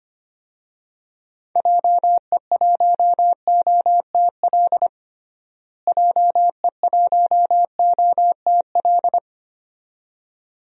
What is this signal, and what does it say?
Morse code, heard as a keyed tone, about 25 wpm, 700 Hz. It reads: JE1OTL JE1OTL
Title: DXCC-100 Call Signs - 6 Characters - Repeated 2x - 25wpm